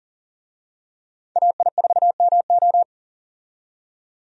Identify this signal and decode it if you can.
Morse code, heard as a keyed tone, about 38 wpm, 700 Hz. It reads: AI4MO
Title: USA Call Signs - 5 Characters - 40wpm